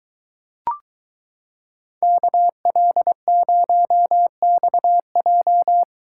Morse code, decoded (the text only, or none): KL0XJ